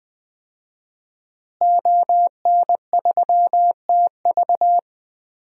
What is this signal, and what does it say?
Morse code, heard as a keyed tone, about 20 wpm, 700 Hz. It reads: ON3TV
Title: DXCC-100 Call Signs - 5 Characters - Repeated 3x - 20wpm